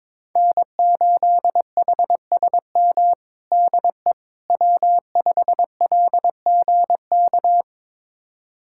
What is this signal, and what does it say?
Morse code, heard as a keyed tone, about 22 wpm, 700 Hz. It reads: N8HSM DE W5LGK
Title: USA Callsign DE Callsign - 5 Characters - 22wpm